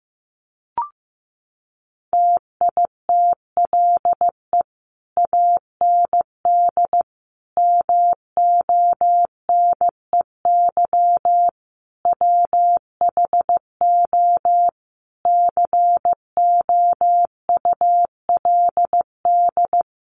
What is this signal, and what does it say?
Morse code, heard as a keyed tone, about 15 wpm, 700 Hz. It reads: TITLE AND MONEY WHO COULD
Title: Sentences from Top 1200 Words 15wpm